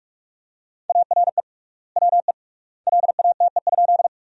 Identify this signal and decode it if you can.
Morse code, heard as a keyed tone, about 46 wpm, 700 Hz. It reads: ARE WE LATE?
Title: 3x Word-Spacing - Sentences from Top 300 Words - 45wpm